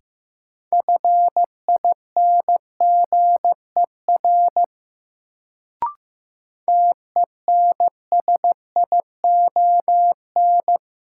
Morse code, heard as a keyed tone, about 15 wpm, 700 Hz. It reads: FINGER TENSION
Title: Sentences from Top 1600 Words - New Word Drill Alt - 15wpm